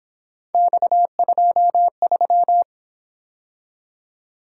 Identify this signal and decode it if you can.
Morse code, heard as a keyed tone, about 27 wpm, 700 Hz. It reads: X23
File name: Speed-Racing - Three Letter-Number Combinations - 22wpm